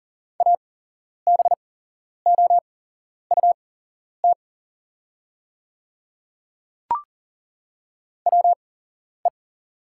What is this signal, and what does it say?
Morse code, heard as a keyed tone, about 40 wpm, spaced as for 11 wpm, 700 Hz. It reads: ABOUT WE
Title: 8x Character Spacing - Top 200 Words - 40wpm